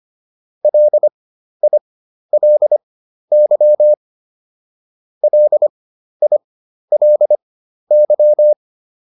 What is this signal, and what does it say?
Morse code, heard as a keyed tone, about 25 wpm, spaced as for 12 wpm, 600 Hz. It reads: LILY LILY